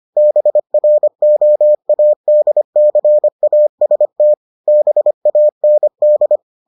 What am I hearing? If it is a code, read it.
Morse code, heard as a keyed tone, about 25 wpm, 600 Hz. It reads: BROADCAST BAND